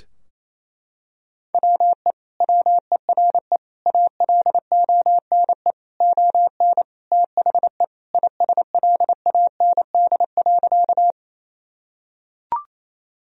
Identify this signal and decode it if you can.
Morse code, heard as a keyed tone, about 28 wpm, 700 Hz. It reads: WE WERE ALONE ON THE ISLAND.
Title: Sentences from Top 1000 Words - Review Part 3 - 28wpm